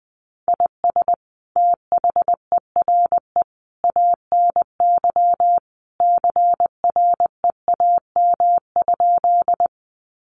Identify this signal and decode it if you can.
Morse code, heard as a keyed tone, about 20 wpm, 700 Hz. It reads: IS THERE ANY CREAM?
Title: Sentences from Top 1500 Words 20wpm